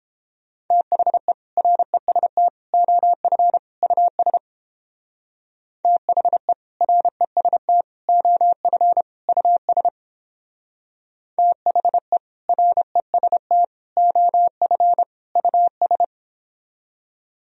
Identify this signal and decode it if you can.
Morse code, heard as a keyed tone, about 29 wpm, 700 Hz. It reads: THE REST OF US THE REST OF US THE REST OF US